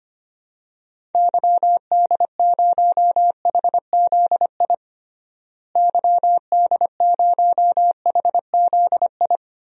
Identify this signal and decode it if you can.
Morse code, heard as a keyed tone, about 25 wpm, 700 Hz. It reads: YD0HZI YD0HZI
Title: DXCC-100 Call Signs - 6 Characters - Repeated 3x - 25wpm